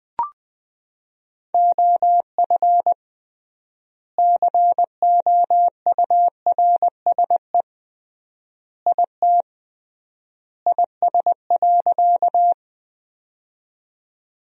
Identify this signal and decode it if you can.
Morse code, heard as a keyed tone, about 20 wpm, 700 Hz. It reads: OF COURSE IT IS.
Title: 3x Word-Spacing - Sentences from Top 500 Words - 20wpm